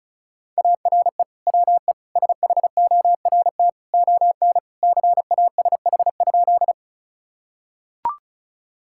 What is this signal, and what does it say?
Morse code, heard as a keyed tone, about 36 wpm, 700 Hz. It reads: ARE WE SHORT ON CASH?
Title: Sentences from Top 1000 Words - Review Part 5 - 35wpm